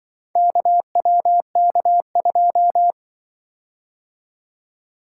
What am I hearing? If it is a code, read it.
Morse code, heard as a keyed tone, about 24 wpm, 700 Hz. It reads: KWK2
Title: Speed-Racing - Four Letter-Number Combinations - 20wpm